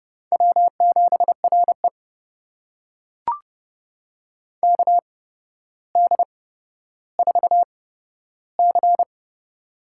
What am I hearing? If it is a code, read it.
Morse code, heard as a keyed tone, about 31 wpm, spaced as for 13 wpm, 700 Hz. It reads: W7RE KD4C
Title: USA Call Signs - 8x Character Spacing - 30wpm